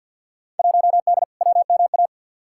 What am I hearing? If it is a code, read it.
Morse code, heard as a keyed tone, about 53 wpm, 700 Hz. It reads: 1D WMA
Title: Field Day Exchanges - 50wpm